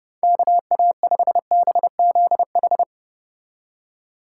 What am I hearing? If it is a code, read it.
Morse code, heard as a keyed tone, about 31 wpm, 700 Hz. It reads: KA5BZH